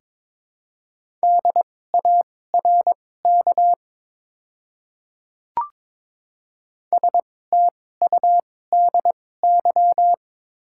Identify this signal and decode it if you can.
Morse code, heard as a keyed tone, about 22 wpm, spaced as for 16 wpm, 700 Hz. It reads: DARK STUDY
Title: 2x Character Spacing - Top 500 Words - 22wpm